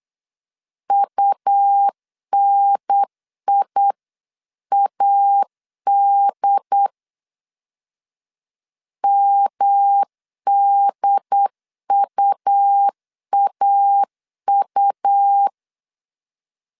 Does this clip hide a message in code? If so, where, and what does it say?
Yes: Morse code from 0.9 to 15.5 s, heard as a keyed tone, about 8 wpm, 800 Hz. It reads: UNIAD MDUAU